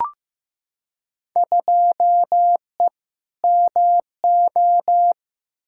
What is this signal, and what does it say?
Morse code, heard as a keyed tone, about 15 wpm, 700 Hz. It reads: 2E MO
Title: Field Day Exchanges - 15wpm